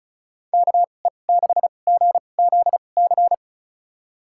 Morse code, heard as a keyed tone, about 36 wpm, spaced as for 25 wpm, 700 Hz. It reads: KE6GZC
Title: USA Call Signs - 6 Characters - 2x Character Spacing - 35wpm